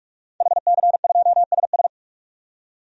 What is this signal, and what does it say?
Morse code, heard as a keyed tone, about 47 wpm, 700 Hz. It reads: HC2SS
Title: DXCC-100 Call Signs - 5 Characters - 45wpm